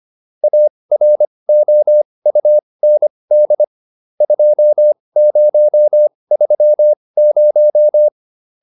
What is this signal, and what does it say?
Morse code, heard as a keyed tone, about 25 wpm, spaced as for 20 wpm, 600 Hz. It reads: AROUND 2030